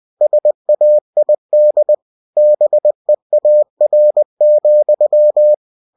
Morse code, heard as a keyed tone, about 20 wpm, 600 Hz. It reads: SAID BEAR,
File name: cw-books / winnie-the-pooh / winnie-20wpm-page-0014